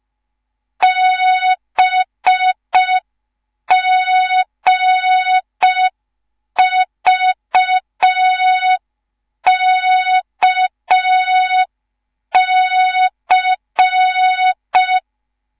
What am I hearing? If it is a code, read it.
Morse code, heard as a keyed tone, about 5 wpm, 3700 Hz. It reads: BGVKC